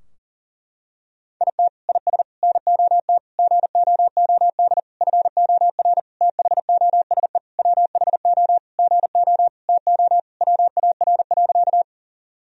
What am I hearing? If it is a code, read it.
Morse code, heard as a keyed tone, about 37 wpm, 700 Hz. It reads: IT IS NOT GOOD FOR THOSE WHO GO TO WAR.
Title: Sentences from Top 1000 Words - Review Part 2 - 40wpm